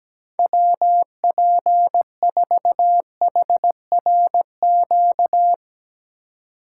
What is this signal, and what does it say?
Morse code, heard as a keyed tone, about 17 wpm, 700 Hz. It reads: WP4HRQ
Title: USA Call Signs - 6 Characters - 2x Character Spacing - 17wpm